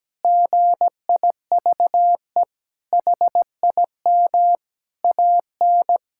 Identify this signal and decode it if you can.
Morse code, heard as a keyed tone, about 17 wpm, 700 Hz. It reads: GIVE HIM AN
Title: Sentences from Top 1000 Words - Review Part 5 - 17wpm